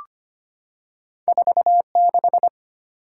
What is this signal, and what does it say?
Morse code, heard as a keyed tone, about 25 wpm, 700 Hz. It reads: 46
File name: Two Letter-Number Combinations - 25wpm